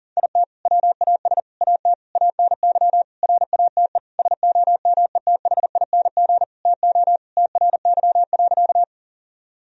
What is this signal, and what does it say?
Morse code, heard as a keyed tone, about 40 wpm, 700 Hz. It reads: IT WAS AT ANY RATE SOMETHING TO TRY.
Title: Sentences from Top 1000 Words - Review Part 4 - 40wpm